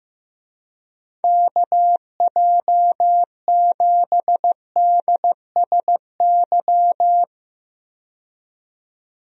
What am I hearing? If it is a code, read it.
Morse code, heard as a keyed tone, about 15 wpm, 700 Hz. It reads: KJ7DSY
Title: USA Call Signs - 6 Characters - 15wpm